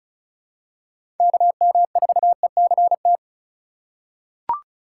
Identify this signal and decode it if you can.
Morse code, heard as a keyed tone, about 34 wpm, 700 Hz. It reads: KM4ECT
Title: USA Call Signs - 6 Characters - 2x Character Spacing - 35wpm